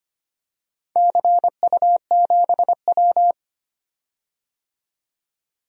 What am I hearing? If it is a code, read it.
Morse code, heard as a keyed tone, about 25 wpm, 700 Hz. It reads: CU7W
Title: DXCC-100 Call Signs - 4 Characters - 25wpm